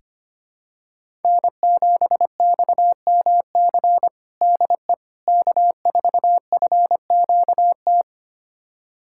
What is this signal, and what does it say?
Morse code, heard as a keyed tone, about 25 wpm, 700 Hz. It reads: N7XMC DE K4FQT
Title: USA Callsign DE Callsign 25wpm